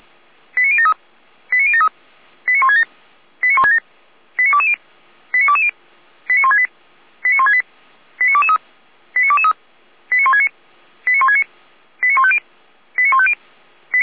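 Five-tone selective call (ZVEI), heard as quick runs of five tones, tones of about 70 ms. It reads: 89073 89073 89167 89167 89220 89220 89168 89168 89223 89223 89169 89169 89150 89150
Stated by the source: ZVEI five-tone selective call received by SDR, server of Feuerwehr Orlamünde fire brigade.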